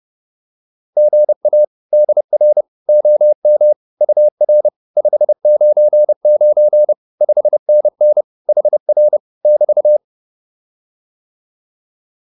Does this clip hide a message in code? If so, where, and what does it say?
Yes: Morse code from 1.0 to 10.0 s, heard as a keyed tone, about 30 wpm, 600 Hz. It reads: GA DR OM UR 599 5NN HR <BT>